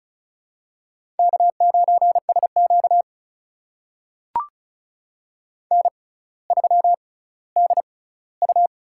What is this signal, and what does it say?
Morse code, heard as a keyed tone, about 34 wpm, spaced as for 18 wpm, 700 Hz. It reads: K9SQ N3DU
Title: USA Call Signs - 4 Characters - 6x Character Spacing - 35wpm